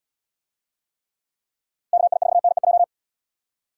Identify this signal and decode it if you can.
Morse code, heard as a keyed tone, about 77 wpm, 700 Hz. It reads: KI4NEJ